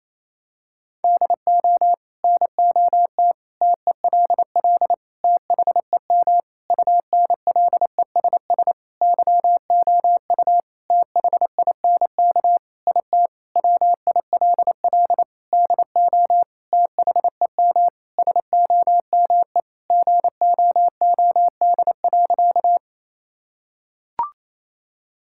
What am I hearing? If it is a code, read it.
Morse code, heard as a keyed tone, about 28 wpm, 700 Hz. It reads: DO NOT TELL THEM UNLESS YOU THINK IT WILL DO THEM SOME GOOD.
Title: Sentences from Top 1000 Words - Review Part 6 - 28wpm